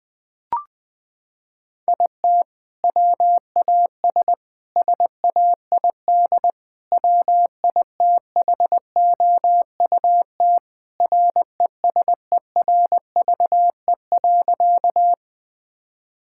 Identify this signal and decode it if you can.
Morse code, heard as a keyed tone, about 21 wpm, 700 Hz. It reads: IT WAS SAID WITHOUT RESERVE.
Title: Sentences from Top 1000 Words - Review Part 6 - 20wpm